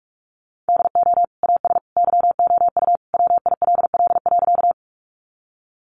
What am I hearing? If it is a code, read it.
Morse code, heard as a keyed tone, about 44 wpm, 700 Hz. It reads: DO AS YOU WILL.
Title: Sentences from Top 100 Words 45wpm